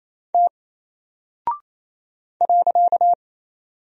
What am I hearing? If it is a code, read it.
Morse code, heard as a keyed tone, about 28 wpm, 700 Hz. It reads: T .
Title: Single Character (Letter:Number:Punctuation:Prosign) Alt 28wpm